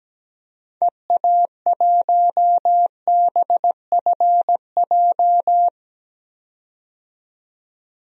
Morse code, heard as a keyed tone, about 17 wpm, 700 Hz. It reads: EA1BFJ
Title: DXCC-100 Call Signs - 6 Characters - 17wpm